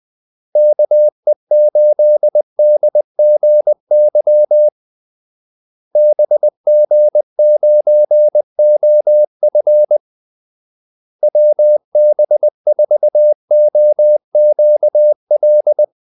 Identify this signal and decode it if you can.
Morse code, heard as a keyed tone, about 20 wpm, 600 Hz. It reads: KE8DGY BG9OF WB4OQL
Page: Kaldesignaler 20 wpm